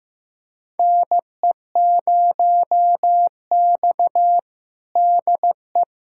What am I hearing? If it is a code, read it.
Morse code, heard as a keyed tone, about 15 wpm, 700 Hz. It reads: NE0X DE
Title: USA Callsign DE Callsign - 4 Characters - 15wpm